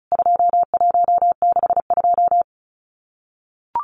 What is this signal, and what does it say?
Morse code, heard as a keyed tone, about 34 wpm, 700 Hz. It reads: 2162
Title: CWT Exchange 35wpm